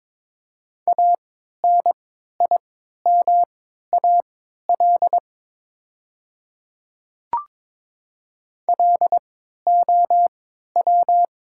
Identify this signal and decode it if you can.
Morse code, heard as a keyed tone, about 22 wpm, spaced as for 12 wpm, 700 Hz. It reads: ANIMAL LOW
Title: 3x Character Spacing - Top 200 Words - 22wpm